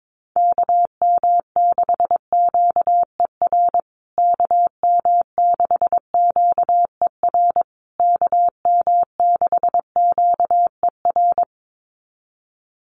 Morse code, heard as a keyed tone, about 22 wpm, 700 Hz. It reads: KM6QER KM6QER KM6QER
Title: USA Call Signs Repeated 3x 22wpm